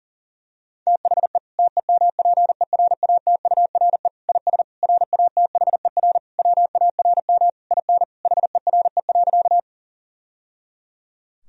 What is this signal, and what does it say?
Morse code, heard as a keyed tone, about 37 wpm, 700 Hz. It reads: THE TEMPERATURE IS RATHER WARM IN HERE.
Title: Sentences from Top 1000 Words - Review Part 4 - 40wpm